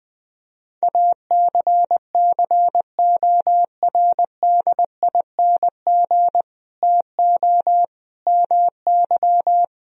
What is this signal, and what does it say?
Morse code, heard as a keyed tone, about 20 wpm, 700 Hz. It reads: ACCORDING TO MY
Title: Sentences from Top 1000 Words - Review Part 4 - 20wpm